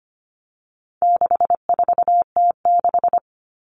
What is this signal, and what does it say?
Morse code, heard as a keyed tone, about 25 wpm, 700 Hz. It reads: 64T6
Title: Common Cut Numbers - 4-Digit 25wpm